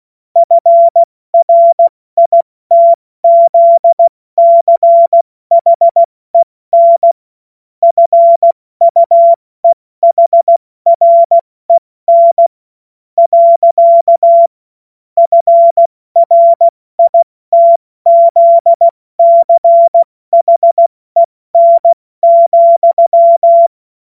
Morse code, heard as a keyed tone, about 16 wpm, spaced as for 14 wpm, 680 Hz. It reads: FRITZCHEN FUEHREN. FRITZCHEN,
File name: Morsekurs / Witze / Witz 14. September 2021, 70 F80